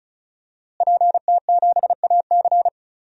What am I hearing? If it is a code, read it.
Morse code, heard as a keyed tone, about 36 wpm, 700 Hz. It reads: PT7AC
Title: DXCC-100 Call Signs - 5 Characters - Repeated 3x - 35wpm